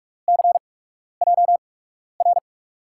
Morse code, heard as a keyed tone, about 44 wpm, spaced as for 12 wpm, 700 Hz. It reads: CJR
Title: USA Call Signs - 5 Characters - 8x Character Spacing - 45wpm